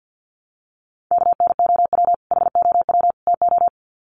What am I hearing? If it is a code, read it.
Morse code, heard as a keyed tone, about 48 wpm, 700 Hz. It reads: KNOW HOW TO
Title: Sets of 3 Words 50wpm